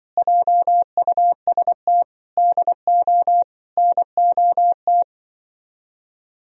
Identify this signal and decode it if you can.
Morse code, heard as a keyed tone, about 24 wpm, 700 Hz. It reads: JUST DO NOT
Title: Speed-Racing - Sets of 4 Words 20wpm